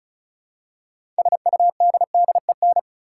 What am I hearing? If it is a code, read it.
Morse code, heard as a keyed tone, about 36 wpm, 700 Hz. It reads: SUDDEN
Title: Six Letter Words - 35wpm